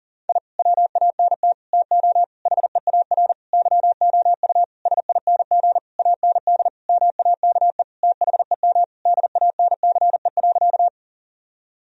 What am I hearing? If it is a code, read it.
Morse code, heard as a keyed tone, about 40 wpm, 700 Hz. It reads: I WANT TO HEAR YOU SING AND MAKE THEM DANCE.
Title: Sentences from Top 1000 Words - Review Part 4 - 40wpm